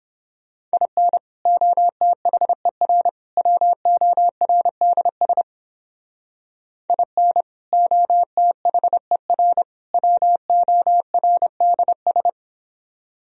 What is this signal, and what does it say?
Morse code, heard as a keyed tone, about 29 wpm, 700 Hz. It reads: IN OTHER WORDS IN OTHER WORDS